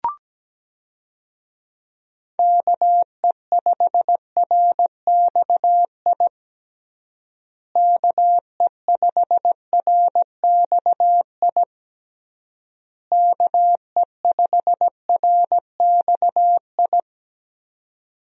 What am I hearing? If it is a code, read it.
Morse code, heard as a keyed tone, about 17 wpm, 700 Hz. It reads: KE5RXI KE5RXI KE5RXI
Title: USA Call Signs - 6 Characters - Repeated 3x - 17wpm